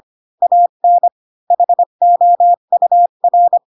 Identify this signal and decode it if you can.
Morse code, heard as a keyed tone, about 25 wpm, 700 Hz. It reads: AN HOUR
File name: Pinocchio / Pinocchio ch8 20 25 ik1hge